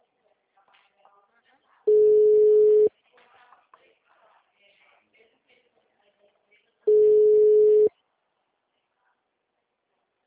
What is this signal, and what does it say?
Ringback tone (Europe, 425 Hz), with speech.
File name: out- -2058- - - .15979